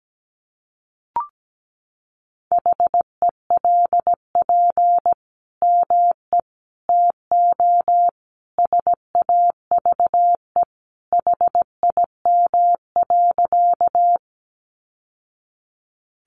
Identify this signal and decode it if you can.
Morse code, heard as a keyed tone, about 17 wpm, 700 Hz. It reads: HELP ME TO SAVE HIM.
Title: Sentences from Top 1100 Words 17wpm